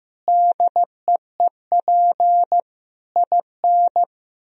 Morse code, heard as a keyed tone, about 15 wpm, 700 Hz. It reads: DEEP IN
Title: Sentences from Top 1000 Words - Review Part 3 - 15wpm